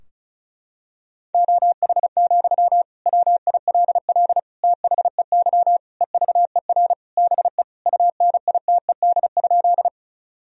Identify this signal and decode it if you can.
Morse code, heard as a keyed tone, about 34 wpm, 700 Hz. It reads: OH, WILL THEY EVER BE UNITED?
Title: Sentences from Top 1000 Words - Review Part 4 - 35wpm